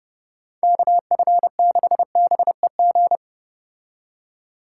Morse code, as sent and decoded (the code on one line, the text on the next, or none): -.- ..-. -.... -... --.
KF6BEG